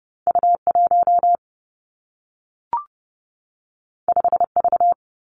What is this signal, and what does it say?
Morse code, heard as a keyed tone, about 30 wpm, 700 Hz. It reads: U 1 5 V
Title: Two Challenging Letter-Number Combinations 30wpm